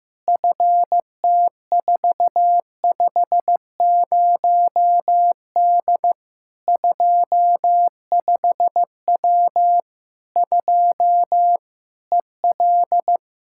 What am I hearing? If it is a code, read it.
Morse code, heard as a keyed tone, about 15 wpm, 700 Hz. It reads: FT450D 25W 2 EL
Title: Practice QSO - Exchange 2 - Info - 15wpm